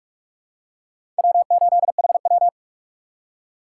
Morse code, heard as a keyed tone, about 43 wpm, 700 Hz. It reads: W8HW
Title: K1USN SST QSO Practice - 45wpm